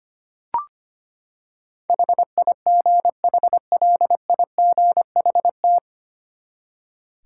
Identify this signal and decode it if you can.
Morse code, heard as a keyed tone, about 25 wpm, 700 Hz. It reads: HIGHLIGHT
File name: Nine Letter Words - 25wpm